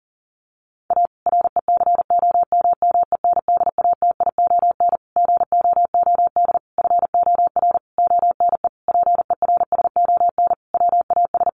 Morse code, heard as a keyed tone, about 40 wpm, 700 Hz. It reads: A RECOMMENDATION GOOD FOR ONE PERSON WAS
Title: Sentences from Top 1400 Words 40wpm